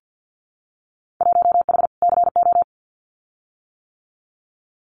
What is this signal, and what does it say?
Morse code, heard as a keyed tone, about 50 wpm, 700 Hz. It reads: 1H CO
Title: Winter Field Day Exchanges 50wpm